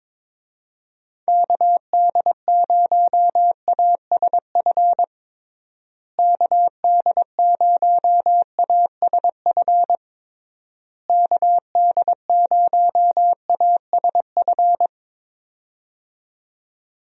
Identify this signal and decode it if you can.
Morse code, heard as a keyed tone, about 22 wpm, 700 Hz. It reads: KD0ASF KD0ASF KD0ASF
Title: USA Call Signs - 6 Characters - Repeated 3x - 22wpm